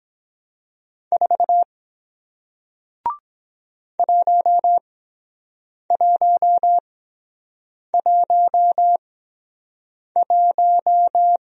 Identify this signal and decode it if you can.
Morse code, heard as a keyed tone, about 23 wpm, spaced as for 7 wpm, 700 Hz. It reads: 4 1111